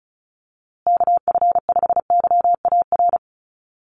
Morse code, heard as a keyed tone, about 34 wpm, 700 Hz. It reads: KF5YAR
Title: USA Call Signs Repeated 3x 35wpm